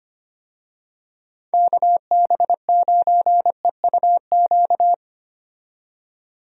Morse code, heard as a keyed tone, about 25 wpm, 700 Hz. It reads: KB9EUQ
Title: USA Call Signs - 6 Characters - 25wpm